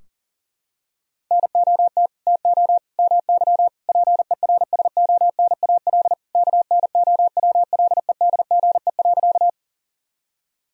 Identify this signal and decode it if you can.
Morse code, heard as a keyed tone, about 40 wpm, 700 Hz. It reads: NOT TO MY PERSONAL KNOWLEDGE.
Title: Sentences from Top 1000 Words - Review Part 4 - 40wpm